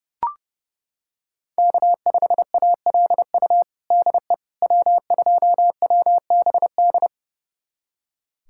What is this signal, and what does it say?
Morse code, heard as a keyed tone, about 30 wpm, 700 Hz. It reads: K5ALU DE W2WBD
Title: USA Callsign DE Callsign - 5 Characters - 30wpm